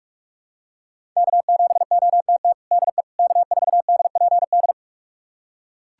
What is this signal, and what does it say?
Morse code, heard as a keyed tone, about 42 wpm, 700 Hz. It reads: K7OTT DE K4DPD
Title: USA Callsign DE Callsign - 5 Characters - 45wpm